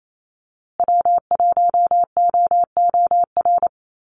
Morse code, heard as a keyed tone, about 28 wpm, 700 Hz. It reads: W1OOR
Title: USA Call Signs Repeated 2x 28wpm